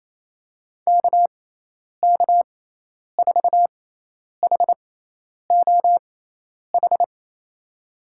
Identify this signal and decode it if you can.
Morse code, heard as a keyed tone, about 27 wpm, spaced as for 10 wpm, 700 Hz. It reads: KK4HOH
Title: USA Call Signs - 6 Characters - 6x Character Spacing - 28wpm